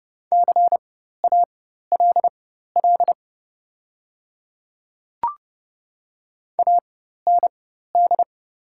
Morse code, heard as a keyed tone, about 30 wpm, spaced as for 14 wpm, 700 Hz. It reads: CALL AND
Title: 4x Character Spacing - Top 100 Words - 30wpm